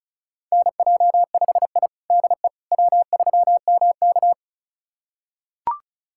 Morse code, heard as a keyed tone, about 34 wpm, 700 Hz. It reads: NJ5I DE W3MK
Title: USA Callsign DE Callsign - 4 Characters - 35wpm